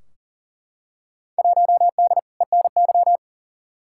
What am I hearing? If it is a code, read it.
Morse code, heard as a keyed tone, about 40 wpm, 700 Hz. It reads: 1D ENY
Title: Field Day Exchanges - 40wpm